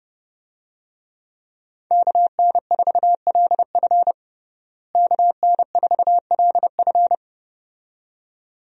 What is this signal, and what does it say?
Morse code, heard as a keyed tone, about 32 wpm, 700 Hz. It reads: KN4LF KN4LF
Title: USA Call Signs - 5 Characters - Repeated 2x - 30wpm